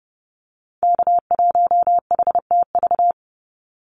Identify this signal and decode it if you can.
Morse code, heard as a keyed tone, about 30 wpm, 700 Hz. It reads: K1HTV
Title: Call Signs - 30wpm